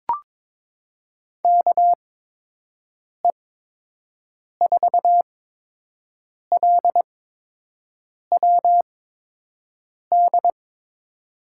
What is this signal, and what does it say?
Morse code, heard as a keyed tone, about 22 wpm, spaced as for 6 wpm, 700 Hz. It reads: KE4LWD
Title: USA Call Signs - 8x Character Spacing - 22wpm